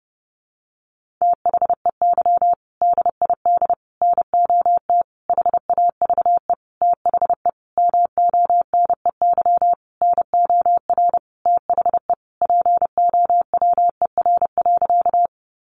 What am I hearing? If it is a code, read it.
Morse code, heard as a keyed tone, about 30 wpm, 700 Hz. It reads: THEY DID NOT HAVE THE MONEY NOR THE POWER.
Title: Sentences from Top 1500 Words 30wpm